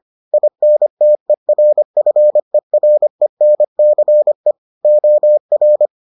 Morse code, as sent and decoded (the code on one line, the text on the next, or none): .. -. - . .-. ..-. .-. -. -.-. / --- .-.
INTERFERENCE OR